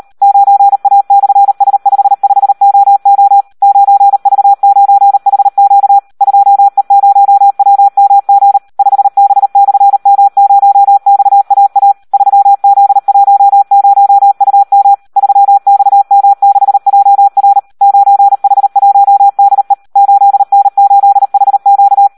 Morse code, heard as a keyed tone, about 38 wpm, 800 Hz. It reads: 9A/S55OO 9V9HQ 2E0WMG 5B/M0XAA 3Z10UM 3XM6JR 9H1DE 8N8HQ
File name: Audio practica CW / SD 1 38